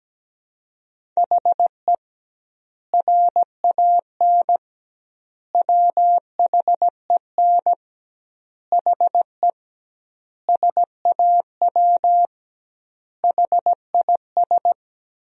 Morse code, heard as a keyed tone, about 17 wpm, 700 Hz. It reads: HE RAN WHEN HE SAW HIS
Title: 2x Word-Spacing - Sentences from Top 500 Words - 17wpm